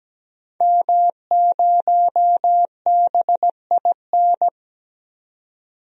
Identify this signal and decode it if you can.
Morse code, heard as a keyed tone, about 17 wpm, 700 Hz. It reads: M0BIN
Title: DXCC-100 Call Signs - 5 Characters - 17wpm